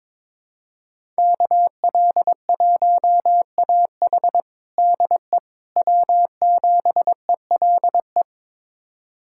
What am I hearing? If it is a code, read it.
Morse code, heard as a keyed tone, about 22 wpm, 700 Hz. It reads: KL1AH DE W7ELE
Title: USA Callsign DE Callsign 22wpm